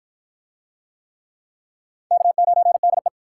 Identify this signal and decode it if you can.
Morse code, heard as a keyed tone, about 53 wpm, 700 Hz. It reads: K9DE